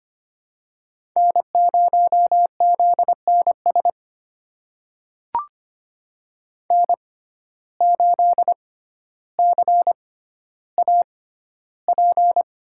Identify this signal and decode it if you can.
Morse code, heard as a keyed tone, about 25 wpm, spaced as for 13 wpm, 700 Hz. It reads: N0ZNS N8CAP
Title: USA Call Signs - 5 Characters - 6x Character Spacing - 25wpm